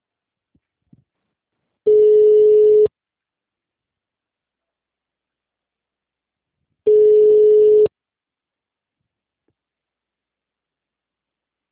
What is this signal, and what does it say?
Ringback tone (Europe, 425 Hz).